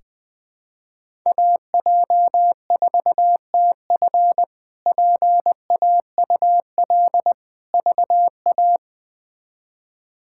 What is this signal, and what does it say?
Morse code, heard as a keyed tone, about 21 wpm, 700 Hz. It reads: AJ4TF PAUL VA
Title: K1USN SST QSO Practice - 20wpm